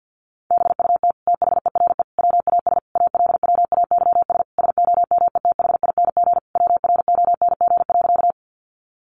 Morse code, heard as a keyed tone, about 50 wpm, 700 Hz. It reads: BUT THERE WAS ALWAYS SOMETHING WRONG.
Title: Sentences from Top 900 Words 50wpm